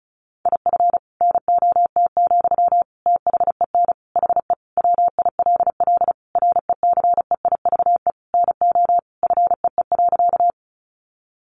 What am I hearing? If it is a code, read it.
Morse code, heard as a keyed tone, about 34 wpm, 700 Hz. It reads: IF NOT, THEN HE WILL RECEIVE NO FEE.
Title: Sentences from Top 1500 Words 35wpm